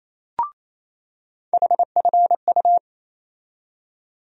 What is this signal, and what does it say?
Morse code, heard as a keyed tone, about 28 wpm, 700 Hz. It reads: HFU
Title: USA Call Sign Postfixes - 3 Letters 28wpm